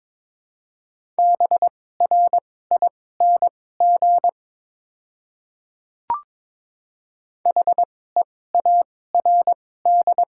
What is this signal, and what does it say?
Morse code, heard as a keyed tone, about 21 wpm, spaced as for 16 wpm, 700 Hz. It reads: BRING HEARD
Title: 2x Character Spacing - Top 500 Words - 22wpm